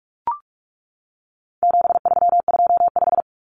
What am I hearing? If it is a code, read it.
Morse code, heard as a keyed tone, about 44 wpm, 700 Hz. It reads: 7325
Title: Numbers - 4-Digit 45wpm